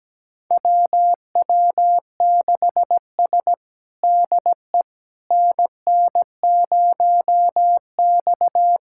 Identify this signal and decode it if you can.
Morse code, heard as a keyed tone, about 17 wpm, 700 Hz. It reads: WW6S DE NN0X
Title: USA Callsign DE Callsign - 4 Characters - 17wpm